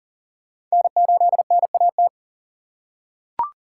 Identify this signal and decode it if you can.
Morse code, heard as a keyed tone, about 40 wpm, 700 Hz. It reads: N8NAT
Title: USA Call Signs - 5 Characters - 4x Character Spacing - 40wpm